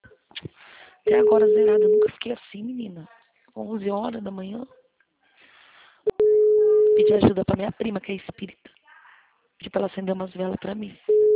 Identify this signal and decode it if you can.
Ringback tone (Europe, 425 Hz), with speech.